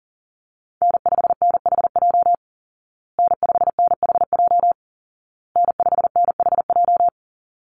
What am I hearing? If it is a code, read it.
Morse code, heard as a keyed tone, about 40 wpm, 700 Hz. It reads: N5NHJ N5NHJ N5NHJ
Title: Call Signs Repeated 2x - 40wpm